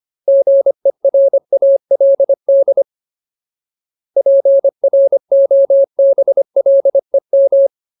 Morse code, heard as a keyed tone, about 25 wpm, 550 Hz. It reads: GERALD PROBLEM
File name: Ultimate Word List / 25WPM Ultimate